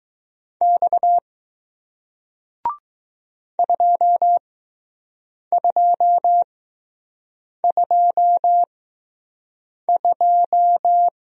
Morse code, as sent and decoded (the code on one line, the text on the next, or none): -..- / ..--- ..--- ..--- ..---
X 2222